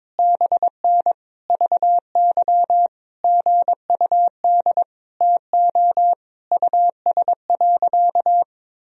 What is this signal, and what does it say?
Morse code, heard as a keyed tone, about 22 wpm, 700 Hz. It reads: BN VY GUD TO US.
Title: Sentences from Top 200 Words and Abbreviations 22wpm